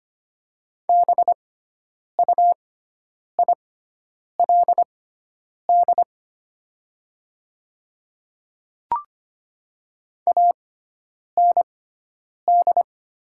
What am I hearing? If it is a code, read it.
Morse code, heard as a keyed tone, about 25 wpm, spaced as for 9 wpm, 700 Hz. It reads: BUILD AND
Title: 6x Character Spacing - Top 500 Words - 25wpm